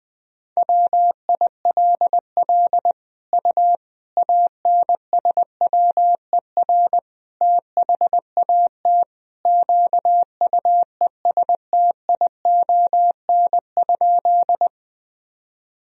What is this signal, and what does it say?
Morse code, heard as a keyed tone, about 20 wpm, 700 Hz. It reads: WILL U ANSWER THAT QUESTION?
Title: Sentences from Top 400 Words and Abbreviations 20wpm